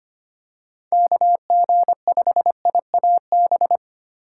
Morse code, heard as a keyed tone, about 25 wpm, 700 Hz. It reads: KG5IAB
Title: USA Call Signs - 6 Characters - 3x Character Spacing - 25wpm